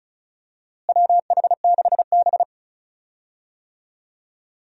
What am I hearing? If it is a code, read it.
Morse code, heard as a keyed tone, about 36 wpm, 700 Hz. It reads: WH6B
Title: USA Call Signs - 4 Characters - Repeated 2x - 35wpm